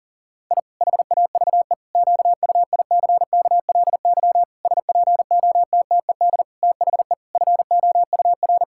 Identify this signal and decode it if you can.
Morse code, heard as a keyed tone, about 40 wpm, 700 Hz. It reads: I HAVE QUICKLY SPOTTED THE FOUR
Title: Pangram Challenge v2 - 40wpm